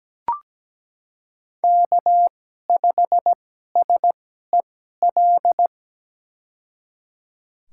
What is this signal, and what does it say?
Morse code, heard as a keyed tone, about 17 wpm, spaced as for 12 wpm, 700 Hz. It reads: K5SEL
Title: USA Call Signs - 2x Character Spacing - 17wpm